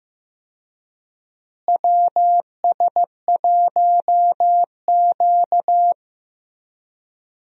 Morse code, heard as a keyed tone, about 15 wpm, 700 Hz. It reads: WS1Q
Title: USA Call Signs - 4 Characters - Repeated 3x - 15wpm